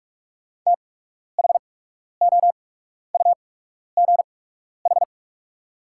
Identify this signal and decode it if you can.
Morse code, heard as a keyed tone, about 46 wpm, spaced as for 12 wpm, 700 Hz. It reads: THOUGH
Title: 8x Character Spacing - Top 500 Words - 45wpm